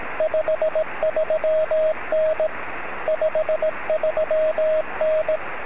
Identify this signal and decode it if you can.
Morse code, heard as a keyed tone, about 18 wpm, 630 Hz, with radio noise. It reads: 53N 53N
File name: sm7lcb / 25-oz9ky-23cm